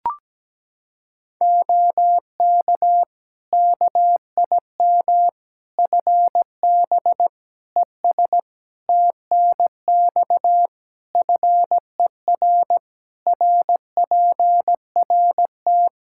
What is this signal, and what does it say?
Morse code, heard as a keyed tone, about 17 wpm, 700 Hz. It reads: OK KIM FB ES TNX FER RPRT
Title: Practice QSO - Exchange 2 - Info - 17wpm